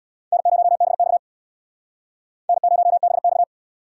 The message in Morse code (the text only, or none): N0BX N0BX